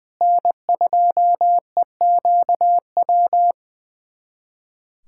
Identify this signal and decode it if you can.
Morse code, heard as a keyed tone, about 20 wpm, 700 Hz. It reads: N2EQW
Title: USA Callsign DE Callsign - 5 Characters - 20wpm